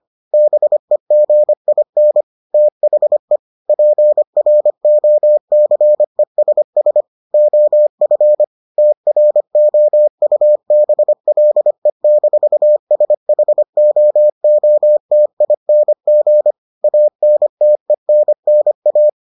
Morse code, heard as a keyed tone, about 25 wpm, 600 Hz. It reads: BEGIN THE PROCESS OF TROUBLE-SHOOTING ANTENNA